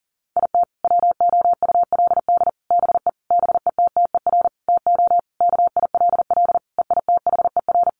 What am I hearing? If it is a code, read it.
Morse code, heard as a keyed tone, about 37 wpm, 700 Hz. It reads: IT WOULD BE BETTER TO KILL EITHER
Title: Sentences from Top 1500 Words 40wpm